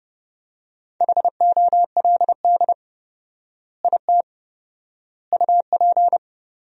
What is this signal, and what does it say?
Morse code, heard as a keyed tone, about 30 wpm, 700 Hz. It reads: HOLD IT UP
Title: 4x Word-Spacing - Sentences from Top 400 Words - 30wpm